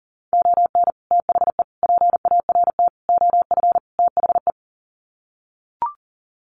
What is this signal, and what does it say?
Morse code, heard as a keyed tone, about 40 wpm, 700 Hz. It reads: ON THE PART OF THE
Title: Sets of 5 Words 40wpm